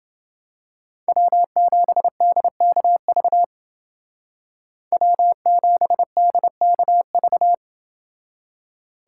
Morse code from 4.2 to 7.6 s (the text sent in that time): W7DKV